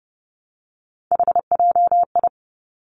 Morse code, heard as a keyed tone, about 30 wpm, 700 Hz. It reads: HJI
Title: Three Challenging Letter-Number Combinations 30wpm